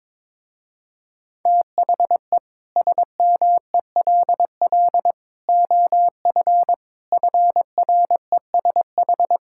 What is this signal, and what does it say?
Morse code, heard as a keyed tone, about 22 wpm, 700 Hz. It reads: THE SMELL OF FRESH
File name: Difficult Sentences with Hints - 22wpm